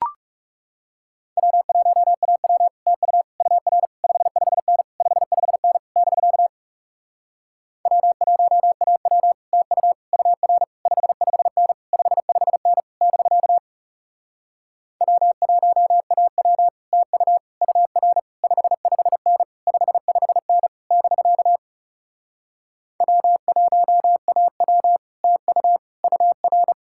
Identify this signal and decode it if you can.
Morse code, heard as a keyed tone, about 37 wpm, 700 Hz. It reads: W1AW TU UR 55N 55N <BK> W1AW TU UR 55N 55N <BK> W1AW TU UR 55N 55N <BK> W1AW TU UR
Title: Speed-Racing Common QSO Phrases Encore - 30wpm